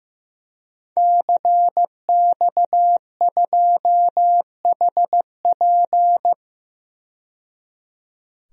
Morse code, heard as a keyed tone, about 15 wpm, 700 Hz. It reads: CX2HP
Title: DXCC-100 Call Signs - 5 Characters - 15wpm